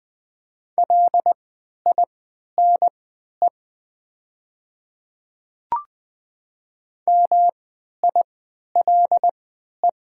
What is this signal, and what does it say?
Morse code, heard as a keyed tone, about 20 wpm, spaced as for 11 wpm, 700 Hz. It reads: LINE MILE
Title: 3x Character Spacing - Top 300 Words - 20wpm